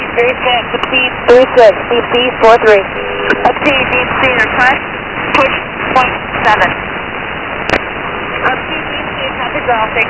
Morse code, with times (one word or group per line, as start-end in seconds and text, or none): none